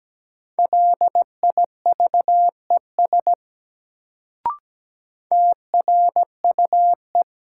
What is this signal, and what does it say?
Morse code, heard as a keyed tone, about 17 wpm, 700 Hz. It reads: LIVES TRUE
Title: Sets of 5 Words Encore - New Word Drill - 17wpm